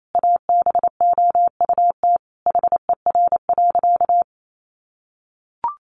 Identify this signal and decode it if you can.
Morse code, heard as a keyed tone, about 28 wpm, 700 Hz. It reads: ABOUT HER.
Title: Sentences from Top 800 Words 28wpm